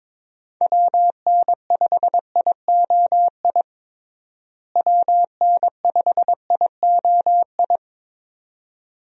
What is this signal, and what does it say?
Morse code, heard as a keyed tone, about 21 wpm, 700 Hz. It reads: WN5IOI WN5IOI
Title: USA Call Signs - 6 Characters - Repeated 2x - 22wpm